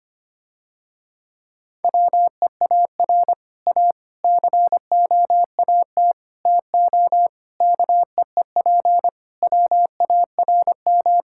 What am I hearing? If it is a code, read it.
Morse code, heard as a keyed tone, about 25 wpm, 700 Hz. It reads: WEAR A COAT TO KEEP WARM
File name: practice / 25wpm / Difficult Sentences with Hints - 25wpm